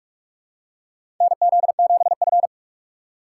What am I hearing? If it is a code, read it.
Morse code, heard as a keyed tone, about 46 wpm, 700 Hz. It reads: NZ7F
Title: USA Call Signs - 4 Characters - 4x Character Spacing - 45wpm